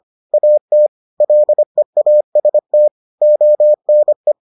Morse code, heard as a keyed tone, about 25 wpm, 600 Hz. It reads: AT LEAST ONE